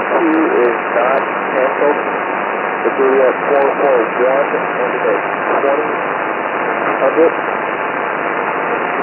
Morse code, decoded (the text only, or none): none